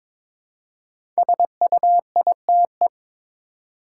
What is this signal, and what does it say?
Morse code, heard as a keyed tone, about 22 wpm, 690 Hz. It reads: SUITE
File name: 5 Letter Words - First 3 Letters Given - 22wpm